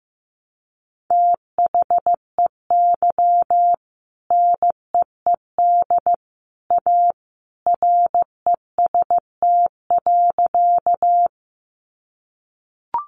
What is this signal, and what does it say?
Morse code, heard as a keyed tone, about 15 wpm, 700 Hz. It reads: THEY NEED A REST.
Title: Sentences from Top 500 Words 15wpm